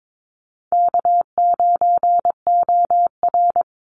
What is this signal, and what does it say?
Morse code, heard as a keyed tone, about 22 wpm, 700 Hz. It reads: K9OR
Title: Call Signs Repeated 2x - 22wpm